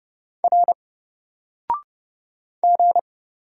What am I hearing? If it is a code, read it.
Morse code, heard as a keyed tone, about 30 wpm, 700 Hz. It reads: R G
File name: Single-Letter-Number-30wpm